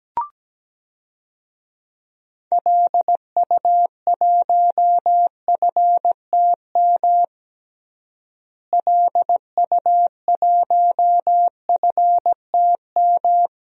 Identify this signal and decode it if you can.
Morse code, heard as a keyed tone, about 17 wpm, 700 Hz. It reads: LU1FTM LU1FTM
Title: DXCC-100 Call Signs - 6 Characters - Repeated 3x - 17wpm